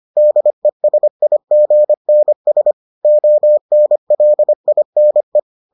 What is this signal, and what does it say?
Morse code, heard as a keyed tone, about 25 wpm, 600 Hz. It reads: DESIGNS ONLINE